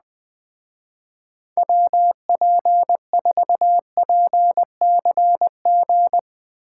Morse code, heard as a keyed tone, about 20 wpm, 700 Hz. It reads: WP4PCG
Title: Speed Racing - USA 2x3 Call Signs 17wpm